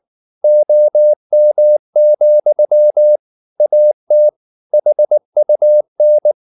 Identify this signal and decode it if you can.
Morse code, heard as a keyed tone, about 19 wpm, 600 Hz. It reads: OM, AT HUN
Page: Ferie 19 wpm